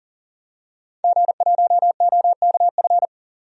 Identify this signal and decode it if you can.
Morse code, heard as a keyed tone, about 40 wpm, 700 Hz. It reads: G1OKF
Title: DXCC-100 Call Signs - 5 Characters - 40wpm